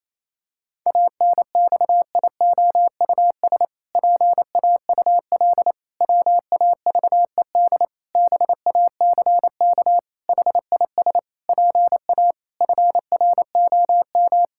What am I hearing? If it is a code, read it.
Morse code, heard as a keyed tone, about 28 wpm, 700 Hz. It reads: ANXIOUS PAUL WAVED BACK HIS PA FROM
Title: Pangram Challenge v2 - 28wpm